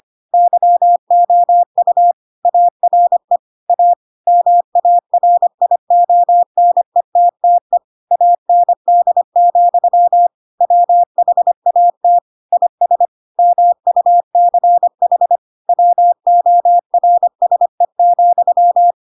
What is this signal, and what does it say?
Morse code, heard as a keyed tone, about 25 wpm, 700 Hz. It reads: YOU ARE A MARIONETTE AND, WHAT IS MUCH WORSE,
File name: Pinocchio / Pinocchio ch4 25 25 ik1hge